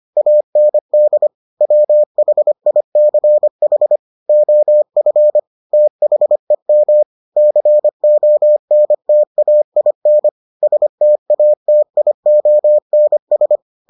Morse code, heard as a keyed tone, about 25 wpm, 600 Hz. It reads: AND WHICH OF THEM CONTAIN STATIONS